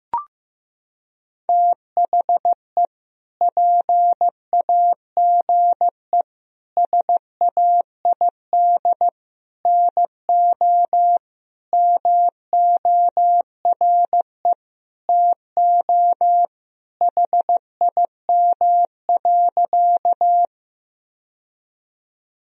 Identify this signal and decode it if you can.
Morse code, heard as a keyed tone, about 15 wpm, 700 Hz. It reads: THE PAGE SAID NO MORE TO HIM.
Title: Sentences from Top 1000 Words - Review Part 1 - 15wpm